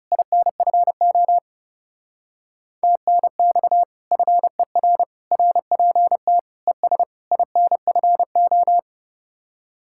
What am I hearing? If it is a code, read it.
Morse code, heard as a keyed tone, about 36 wpm, 700 Hz. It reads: INFO TNX FER RPT ES INFO
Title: Speed-Racing Common QSO Phrases - 30wpm